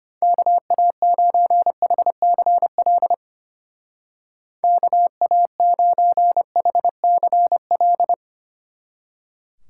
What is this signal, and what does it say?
Morse code, heard as a keyed tone, about 31 wpm, 700 Hz. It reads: KA9HCL KA9HCL
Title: Speed Racing - USA 2x3 Call Signs 25wpm